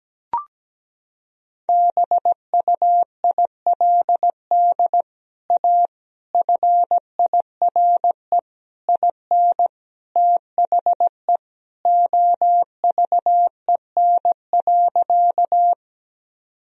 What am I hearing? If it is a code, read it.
Morse code, heard as a keyed tone, about 17 wpm, 700 Hz. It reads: BUILD A FIRE IN THE OVEN.
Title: Sentences from Top 1000 Words - Review Part 4 - 17wpm